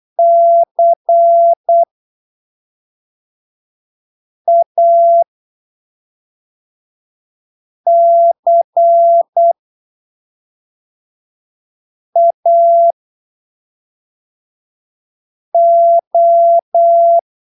Morse code, heard as a keyed tone, about 8 wpm, spaced as for 3 wpm, 680 Hz. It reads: CACAO
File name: arquivos / texto / texto 01 08ppm